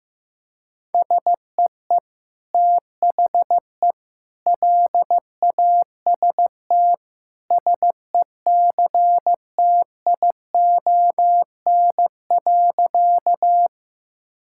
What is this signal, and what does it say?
Morse code, heard as a keyed tone, about 16 wpm, 700 Hz. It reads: SEE THE LAST SECTION.
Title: Sentences from Top 1000 Words - Review Part 5 - 15wpm